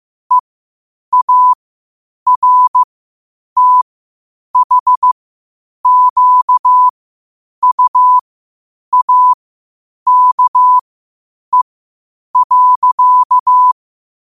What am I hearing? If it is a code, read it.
Morse code, heard as a keyed tone, about 15 wpm, spaced as for 8 wpm, 1000 Hz. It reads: EARTHQUAKE.
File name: 8wpm